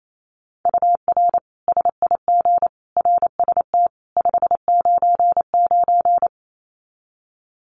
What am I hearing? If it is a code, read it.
Morse code, heard as a keyed tone, about 28 wpm, 700 Hz. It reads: UR SIG RST 599
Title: Common QSO Phrases 28wpm